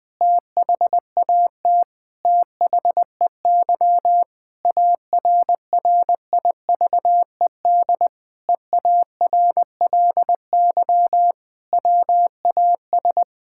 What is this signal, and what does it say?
Morse code, heard as a keyed tone, about 20 wpm, 700 Hz. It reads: THAT THEY ARRIVED EARLY WAS
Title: Sentences with 4-Word Phrases - 20wpm